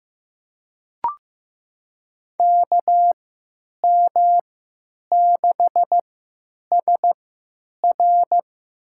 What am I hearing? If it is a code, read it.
Morse code, heard as a keyed tone, about 15 wpm, spaced as for 8 wpm, 700 Hz. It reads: KM6SR
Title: USA Call Signs - 5 Characters - 3x Character Spacing - 15wpm